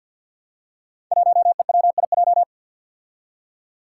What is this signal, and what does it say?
Morse code, heard as a keyed tone, about 49 wpm, 700 Hz. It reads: 1EWIJ
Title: Five Challenging Letter-Number Combinations 50wpm